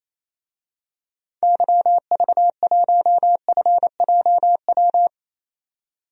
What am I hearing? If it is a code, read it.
Morse code, heard as a keyed tone, about 28 wpm, 700 Hz. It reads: YV1FJW
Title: DXCC-100 Call Signs - 6 Characters - 28wpm